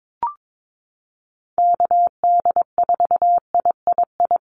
Morse code, heard as a keyed tone, about 22 wpm, 700 Hz. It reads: KD4III
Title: USA Call Signs Repeated 2x 22wpm